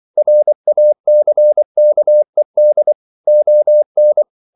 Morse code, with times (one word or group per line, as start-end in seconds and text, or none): 0.2-2.9 RACKED
3.3-4.2 ON